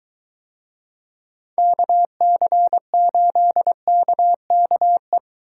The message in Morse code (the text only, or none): KC8KKE